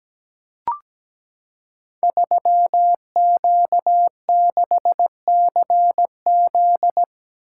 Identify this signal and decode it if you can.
Morse code, heard as a keyed tone, about 17 wpm, 700 Hz. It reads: 3Q6CZ
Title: Code Groups - 5 Letter:Numbers - 17wpm